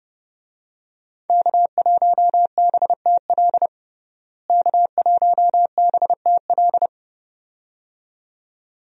Morse code, heard as a keyed tone, about 32 wpm, 700 Hz. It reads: K1BTL K1BTL
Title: USA Call Signs - 5 Characters - Repeated 2x - 30wpm